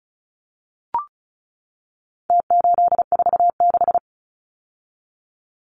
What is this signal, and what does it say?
Morse code, heard as a keyed tone, about 34 wpm, 700 Hz. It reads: T846
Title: Numbers - Contest Style - 4-Digit 35wpm